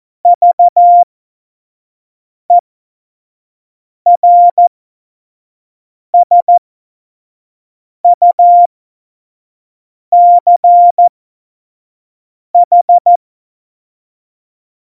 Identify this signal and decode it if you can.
Morse code, heard as a keyed tone, about 14 wpm, spaced as for 5 wpm, 700 Hz. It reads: VERSUCH
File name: Morsekurs / Witze / Witz 22. Dezember 2020, 25 F70